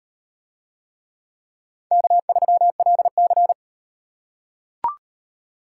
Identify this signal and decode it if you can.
Morse code, heard as a keyed tone, about 35 wpm, 700 Hz. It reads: K3LC